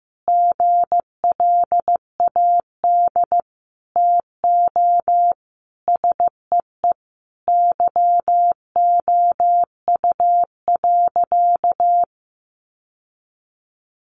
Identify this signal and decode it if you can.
Morse code, heard as a keyed tone, about 14 wpm, 700 Hz. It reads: GLAD TO SEE YOU.
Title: Sentences from Top 800 Words 15wpm